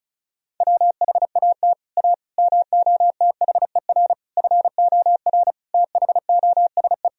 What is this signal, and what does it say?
Morse code, heard as a keyed tone, about 34 wpm, 700 Hz. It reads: WHAT A MOTHER FOR THOSE
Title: Sentences from Top 1000 Words - Review Part 1 - 35wpm